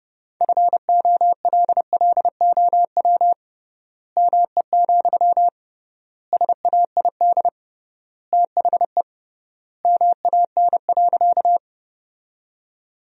Morse code, heard as a keyed tone, about 30 wpm, 700 Hz. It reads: FOLLOW ME, SAID THE MAN.
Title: 3x Word-Spacing - Sentences from Top 200 Words - 30wpm